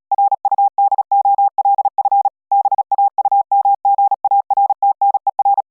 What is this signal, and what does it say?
Morse code, heard as a keyed tone, about 40 wpm, 800 Hz. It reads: RUDOLF BAUMGARTNER